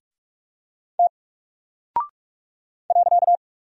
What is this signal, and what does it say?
Morse code, heard as a keyed tone, about 45 wpm, 700 Hz. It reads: T .
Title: Single Character (Letter:Number:Punctuation:Prosign) 45wpm